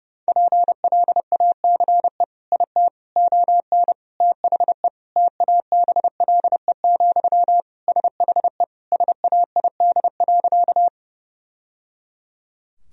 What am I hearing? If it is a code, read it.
Morse code, heard as a keyed tone, about 32 wpm, 700 Hz. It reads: PLACE IT ON THE TABLE, SHE SAID.
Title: Sentences from Top 1000 Words - Review Part 2 - 30wpm